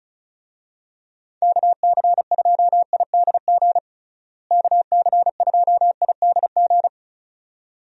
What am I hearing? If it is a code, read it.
Morse code, heard as a keyed tone, about 36 wpm, 700 Hz. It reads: KC2IDG KC2IDG
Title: USA Call Signs - 6 Characters - Repeated 2x - 35wpm